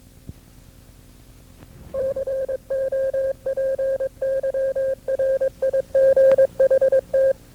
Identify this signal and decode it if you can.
Morse code, heard as a keyed tone, about 22 wpm, 530 Hz. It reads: COPYRIGHT
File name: FreeFiles / CW-practice / KB6MT-tape2-side1